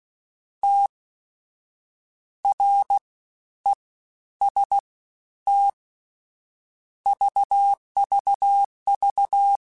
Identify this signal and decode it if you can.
Morse code, heard as a keyed tone, about 16 wpm, 800 Hz. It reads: T R E S T VVV
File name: Combinations / Combinations-mp3s / Combinations@15